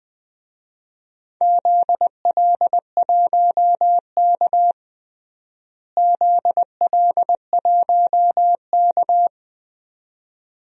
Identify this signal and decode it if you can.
Morse code, heard as a keyed tone, about 21 wpm, 700 Hz. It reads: ZL1K ZL1K
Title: DXCC-100 Call Signs - 4 Characters - Repeated 2x - 20wpm